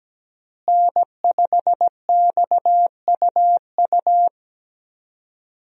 Morse code, heard as a keyed tone, about 17 wpm, 700 Hz. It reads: N5XUU
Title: USA Call Signs - 5 Characters - Repeated 2x - 17wpm